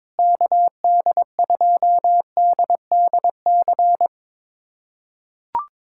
Morse code, heard as a keyed tone, about 22 wpm, 700 Hz. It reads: KD2DDC
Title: USA Callsign DE Callsign 22wpm